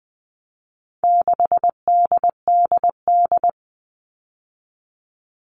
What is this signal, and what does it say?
Morse code, heard as a keyed tone, about 20 wpm, 700 Hz. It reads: 6DDD
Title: All Cut Numbers - 4-Digit 20wpm